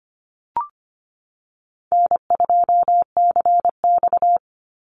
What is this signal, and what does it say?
Morse code, heard as a keyed tone, about 25 wpm, 700 Hz. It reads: N2CX
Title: Call Signs Repeated 2x - 25wpm